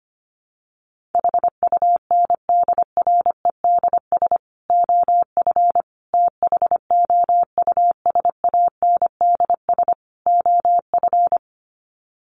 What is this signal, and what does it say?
Morse code, heard as a keyed tone, about 25 wpm, 700 Hz. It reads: HUNDREDS OF THOUSANDS OF
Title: Sets of 4 Words 25wpm